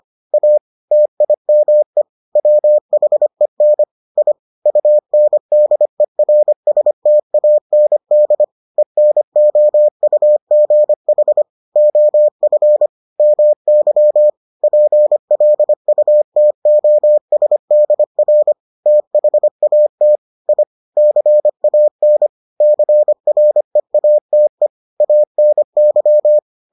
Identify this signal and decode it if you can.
Morse code, heard as a keyed tone, about 25 wpm, 600 Hz. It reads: A TIME WHEN I UNDERSTAND ENOUGH OF MY PLUTOSDR THAT I CAN CREATE ANY